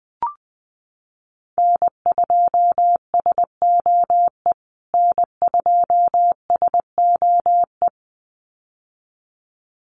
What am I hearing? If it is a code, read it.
Morse code, heard as a keyed tone, about 20 wpm, 700 Hz. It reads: N2SOE N2SOE
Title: USA Call Signs Repeated 2x 20wpm